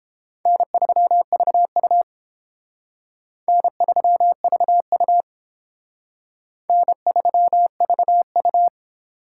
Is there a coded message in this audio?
Yes: Morse code from 0.5 to 8.7 s, heard as a keyed tone, about 31 wpm, 700 Hz. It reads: N3VU N3VU N3VU